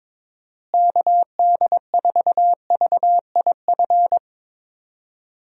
Morse code, heard as a keyed tone, about 22 wpm, 700 Hz. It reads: KD4VIF
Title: USA Call Signs - 6 Characters - Repeated 3x - 22wpm